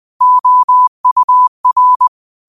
Morse code, heard as a keyed tone, about 20 wpm, 1000 Hz. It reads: OUR